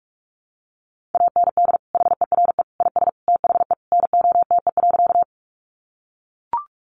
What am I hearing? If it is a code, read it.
Morse code, heard as a keyed tone, about 44 wpm, 700 Hz. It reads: AND HERE IS THE NOTE.
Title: Sentences from Top 500 Words 45wpm